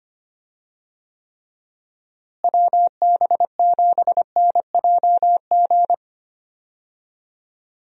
Morse code, heard as a keyed tone, about 25 wpm, 700 Hz. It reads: WB7NJG
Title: USA Call Signs - 6 Characters - Repeated 3x - 25wpm